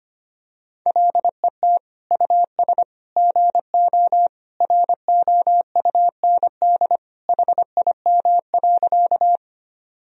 Morse code, heard as a keyed tone, about 25 wpm, 700 Hz. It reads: LET US GO ROUND HIM.
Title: Sentences from Top 1000 Words - Review Part 1 - 25wpm